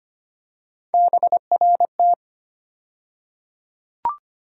Morse code, heard as a keyed tone, about 25 wpm, 700 Hz. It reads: BRT
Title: USA Call Sign Postfixes - 3 Letters 25wpm